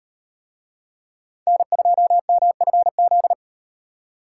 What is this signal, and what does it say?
Morse code, heard as a keyed tone, about 36 wpm, 700 Hz. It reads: N2MFZ